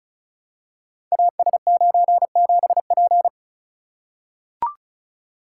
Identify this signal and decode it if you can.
Morse code, heard as a keyed tone, about 36 wpm, 700 Hz. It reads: AS97P
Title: Code Groups - 5 Letter:Numbers - 35wpm